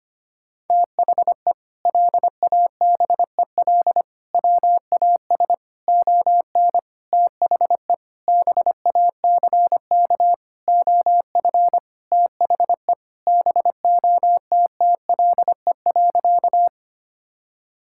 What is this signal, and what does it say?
Morse code, heard as a keyed tone, about 25 wpm, 700 Hz. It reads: THE LABEL WAS ON THE BACK OF THE BOTTLE.
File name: practice / 25wpm / Sentences with 4-Word Phrases - 25wpm